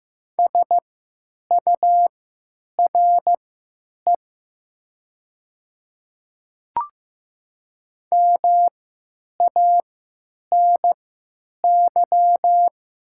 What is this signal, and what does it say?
Morse code, heard as a keyed tone, about 15 wpm, spaced as for 9 wpm, 700 Hz. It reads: SURE MANY
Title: 3x Character Spacing - Top 500 Words - 15wpm